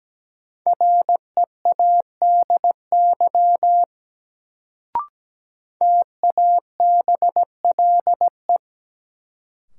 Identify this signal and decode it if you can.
Morse code, heard as a keyed tone, about 17 wpm, 700 Hz. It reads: READY TABLE
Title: Sentences from Top 400 Words - New Word Drill - 17wpm